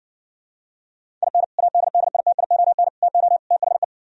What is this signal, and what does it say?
Morse code, heard as a keyed tone, about 60 wpm, 700 Hz. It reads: IN ADDITION TO THE